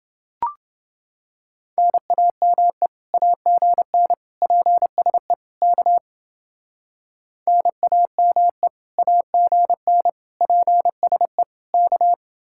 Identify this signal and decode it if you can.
Morse code, heard as a keyed tone, about 29 wpm, 700 Hz. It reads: NAME AGN PSE K NAME AGN PSE K